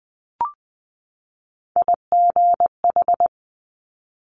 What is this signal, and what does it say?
Morse code, heard as a keyed tone, about 20 wpm, 700 Hz. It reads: IGH
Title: Top Three Letter Combinations in Words 20wpm